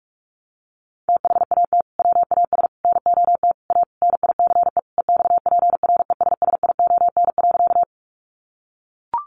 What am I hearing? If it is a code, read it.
Morse code, heard as a keyed tone, about 44 wpm, 700 Hz. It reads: THAT WAS NOT A NICE EXPRESSION.
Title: Sentences from Top 1200 Words 45wpm